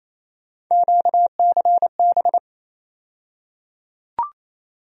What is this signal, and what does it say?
Morse code, heard as a keyed tone, about 28 wpm, 700 Hz. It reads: QCB
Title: USA Call Sign Postfixes - 3 Letters 28wpm